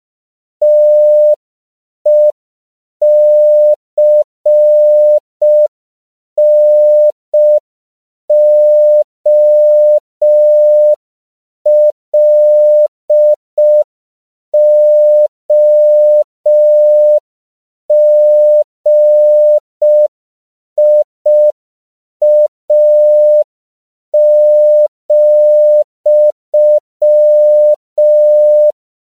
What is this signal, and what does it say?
Morse code, heard as a keyed tone, about 5 wpm, 600 Hz. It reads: TECNOLOGIA,
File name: CW / 5PPM